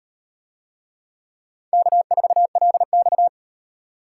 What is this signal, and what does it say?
Morse code, heard as a keyed tone, about 38 wpm, 700 Hz. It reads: K4LX